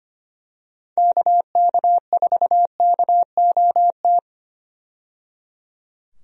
Morse code, heard as a keyed tone, about 25 wpm, 700 Hz. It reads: KK4KOT
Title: USA Call Signs - 6 Characters - 25wpm